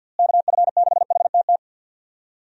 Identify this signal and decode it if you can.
Morse code, heard as a keyed tone, about 52 wpm, 700 Hz. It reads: KF6HTT
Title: USA Call Signs - 6 Characters - Repeated 3x - 50wpm